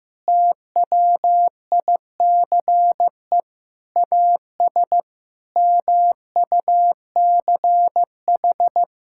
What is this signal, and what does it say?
Morse code, heard as a keyed tone, about 16 wpm, 700 Hz. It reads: TWICE AS MUCH
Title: Sentences from Top 1600 Words 15wpm